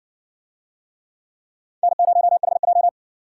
Morse code, heard as a keyed tone, about 60 wpm, 700 Hz. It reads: N0HJ